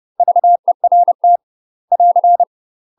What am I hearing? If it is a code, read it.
Morse code, heard as a keyed tone, about 30 wpm, 700 Hz. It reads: VERT <AR>
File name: UK MASTER QSOs TRAINING / 30WPM / UK MASTER QSOs 0016